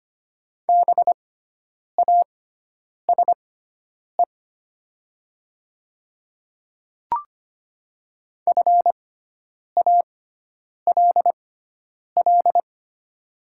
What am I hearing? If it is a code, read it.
Morse code, heard as a keyed tone, about 25 wpm, spaced as for 9 wpm, 700 Hz. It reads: BASE FALL
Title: 6x Character Spacing - Top 500 Words - 25wpm